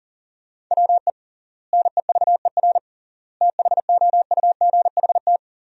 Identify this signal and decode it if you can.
Morse code, heard as a keyed tone, about 40 wpm, 700 Hz. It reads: WE NEVER THOUGHT
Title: 3x Word-Spacing - Sentences from Top 300 Words - 40wpm